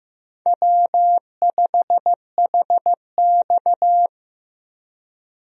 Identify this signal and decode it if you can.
Morse code, heard as a keyed tone, about 15 wpm, 700 Hz. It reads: W5HX
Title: USA Call Signs - 4 Characters - 15wpm